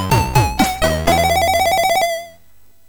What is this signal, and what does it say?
There is no code.